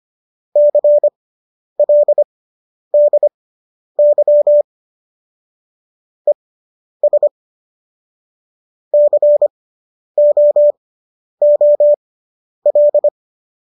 Morse code, heard as a keyed tone, about 25 wpm, spaced as for 10 wpm, 590 Hz. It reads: CLDY ES COOL